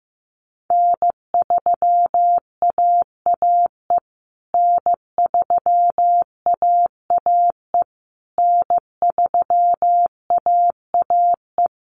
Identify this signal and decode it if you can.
Morse code, heard as a keyed tone, about 15 wpm, 700 Hz. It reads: N3AAE N3AAE N3AAE
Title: USA Call Signs Repeated 3x 15wpm